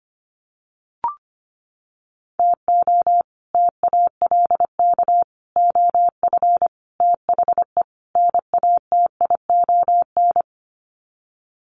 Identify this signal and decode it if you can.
Morse code, heard as a keyed tone, about 25 wpm, 700 Hz. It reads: TO TALK OF THE NATION
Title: Sets of 5 Words 25wpm